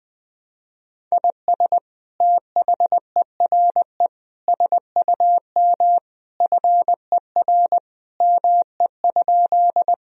Morse code, heard as a keyed tone, about 20 wpm, 700 Hz. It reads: IS THERE SUM FER ME?
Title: Sentences from Top 200 Words and Abbreviations 20wpm